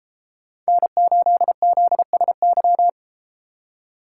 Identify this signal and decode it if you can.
Morse code, heard as a keyed tone, about 32 wpm, 700 Hz. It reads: N8ZSY